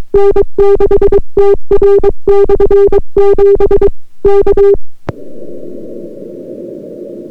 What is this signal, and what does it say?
Morse code, heard as a keyed tone, about 22 wpm, 400 Hz. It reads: N6TR/7 K